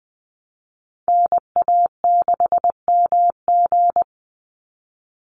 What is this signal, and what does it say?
Morse code, heard as a keyed tone, about 20 wpm, 700 Hz. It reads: NA6MG
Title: NA SOTA Chaser Callsigns 20wpm